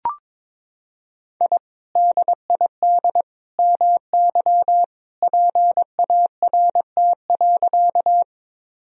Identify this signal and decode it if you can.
Morse code, heard as a keyed tone, about 22 wpm, 700 Hz. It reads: I DID MY PART.
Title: Sentences from Top 1000 Words - Review Part 1 - 22wpm